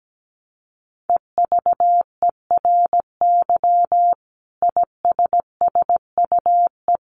Morse code, heard as a keyed tone, about 17 wpm, 700 Hz. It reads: EVERY ISSUE
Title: Sentences from Top 1400 Words 17wpm